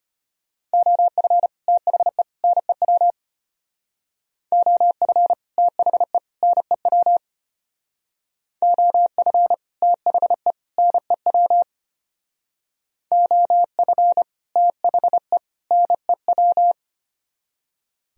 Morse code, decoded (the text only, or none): OF THE NEW OF THE NEW OF THE NEW OF THE NEW